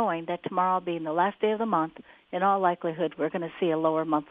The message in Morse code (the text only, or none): none